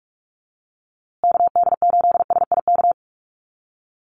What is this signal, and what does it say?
Morse code, heard as a keyed tone, about 44 wpm, 700 Hz. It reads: KD8SIK
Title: USA Call Signs Repeated 3x 45wpm